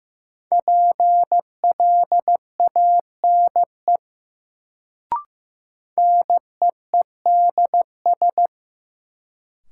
Morse code, heard as a keyed tone, about 15 wpm, 700 Hz. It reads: PLANE NEEDS
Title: Sentences from Top 1000 Words - Review Part 3 - New Word Drill - 15wpm